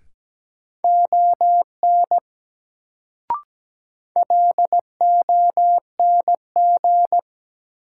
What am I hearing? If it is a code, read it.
Morse code, heard as a keyed tone, about 17 wpm, 700 Hz. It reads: ON LONG
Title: Sentences from Top 100 Words - New Word Drill - 17wpm